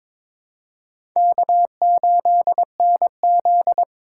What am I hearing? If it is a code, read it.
Morse code, heard as a keyed tone, about 22 wpm, 700 Hz. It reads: K8NZ
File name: Four Letter-Number Combinations - 22wpm